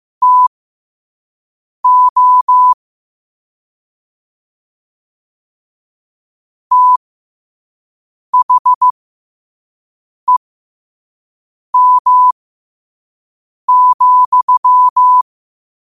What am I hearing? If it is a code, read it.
Morse code, heard as a keyed tone, about 15 wpm, spaced as for 5 wpm, 1000 Hz. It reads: TO THEM,
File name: qotdfiles / 5wpm